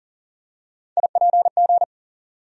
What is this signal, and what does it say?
Morse code, heard as a keyed tone, about 40 wpm, 700 Hz. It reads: IPG
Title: USA Call Sign Postfixes - 3 Letters 40wpm